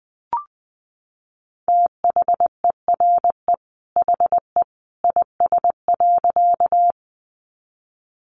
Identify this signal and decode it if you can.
Morse code, heard as a keyed tone, about 20 wpm, 700 Hz. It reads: THERE HE IS.
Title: A Sound Of Thunder 20wpm